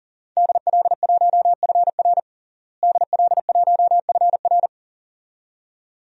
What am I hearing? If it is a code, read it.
Morse code, heard as a keyed tone, about 40 wpm, 700 Hz. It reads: DL1FR DL1FR
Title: DXCC-100 Call Signs - 5 Characters - Repeated 2x - 40wpm